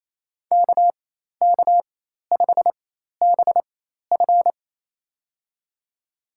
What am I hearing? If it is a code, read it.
Morse code, heard as a keyed tone, about 28 wpm, spaced as for 13 wpm, 700 Hz. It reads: KK5BF
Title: USA Call Signs - 5 Characters - 4x Character Spacing - 28wpm